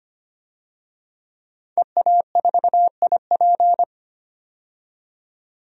Morse code, heard as a keyed tone, about 25 wpm, 700 Hz. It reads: EA4IP
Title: DXCC-100 Call Signs - 5 Characters - 25wpm